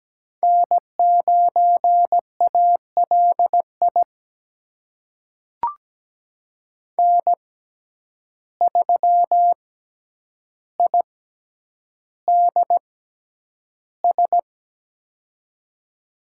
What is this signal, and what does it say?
Morse code, heard as a keyed tone, about 17 wpm, spaced as for 9 wpm, 700 Hz. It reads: N9ALI N3IDS
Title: USA Call Signs - 5 Characters - 6x Character Spacing - 17wpm